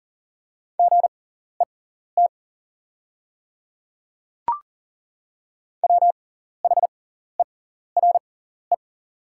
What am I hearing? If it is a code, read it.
Morse code, heard as a keyed tone, about 40 wpm, spaced as for 14 wpm, 700 Hz. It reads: GET WHERE
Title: 6x Character Spacing - Top 300 Words - 40wpm